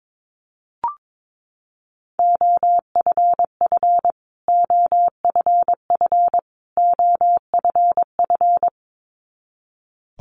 Morse code, heard as a keyed tone, about 22 wpm, 700 Hz. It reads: OFF OFF OFF
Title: Top 500 Words Repeated 22wpm